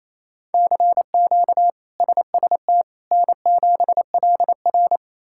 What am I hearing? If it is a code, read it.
Morse code, heard as a keyed tone, about 28 wpm, 700 Hz. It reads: CQ SST N7LR
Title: K1USN SST QSO Practice - 28wpm